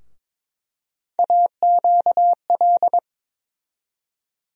Morse code, heard as a keyed tone, about 22 wpm, 700 Hz. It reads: AQL
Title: USA Call Sign Postfixes - 3 Letters 22wpm